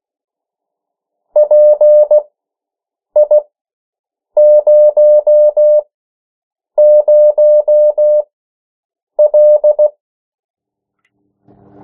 Morse code, heard as a keyed tone, about 16 wpm, spaced as for 7 wpm, 600 Hz. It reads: PI00L
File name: morse mp3 / 14 urok 13